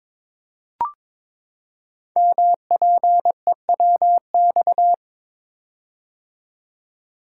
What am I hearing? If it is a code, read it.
Morse code, heard as a keyed tone, about 23 wpm, 700 Hz. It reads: MPEWX
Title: Code Groups - 5 Letters - 22wpm